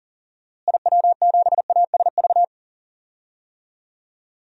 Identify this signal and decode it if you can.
Morse code, heard as a keyed tone, about 40 wpm, 700 Hz. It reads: IW7ASV
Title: DXCC-100 Call Signs - 6 Characters - Repeated 2x - 40wpm